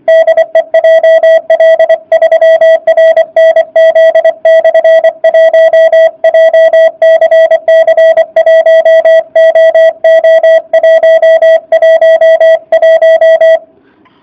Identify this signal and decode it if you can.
Morse code, heard as a keyed tone, about 24 wpm, 650 Hz. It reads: DE JL3RNZ/1 JCC 1OO111